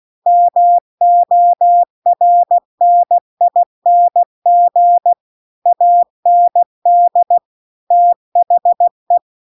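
Morse code, heard as a keyed tone, about 16 wpm, 700 Hz. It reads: MORNING AND THE
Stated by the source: Morse practice text at 16 WPM.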